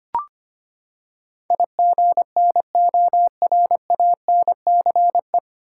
Morse code, heard as a keyed tone, about 25 wpm, 700 Hz. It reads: IGNORANCE
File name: Nine Letter Words - 25wpm